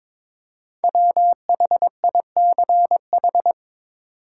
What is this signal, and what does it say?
Morse code, heard as a keyed tone, about 22 wpm, 690 Hz. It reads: WHICH
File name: 5 Letter Words - First 3 Letters Given - 22wpm